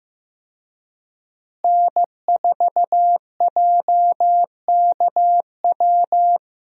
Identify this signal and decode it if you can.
Morse code, heard as a keyed tone, about 16 wpm, 700 Hz. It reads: N4JKW
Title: USA Call Signs - 5 Characters - Repeated 3x - 15wpm